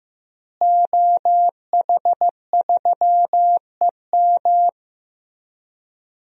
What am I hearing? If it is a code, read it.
Morse code, heard as a keyed tone, about 15 wpm, 700 Hz. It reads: OH3EM
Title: DXCC-100 Call Signs - 5 Characters - Repeated 3x - 15wpm